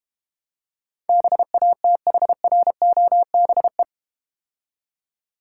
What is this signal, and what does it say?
Morse code, heard as a keyed tone, about 31 wpm, 690 Hz. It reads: BATHROBE
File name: practice / 25wpm / Difficult Words - Speed-Racing with Hints - 25wpm